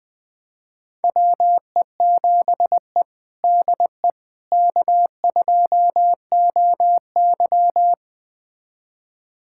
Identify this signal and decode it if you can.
Morse code, heard as a keyed tone, about 20 wpm, 700 Hz. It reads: WE7E DE K2OY
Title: USA Callsign DE Callsign - 4 Characters - 20wpm